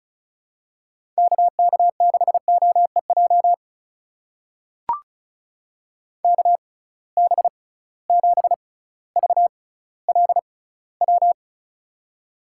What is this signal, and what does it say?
Morse code, heard as a keyed tone, about 36 wpm, spaced as for 18 wpm, 700 Hz. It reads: KK6OEJ KB7VLW
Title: USA Call Signs - 6 Characters - 6x Character Spacing - 35wpm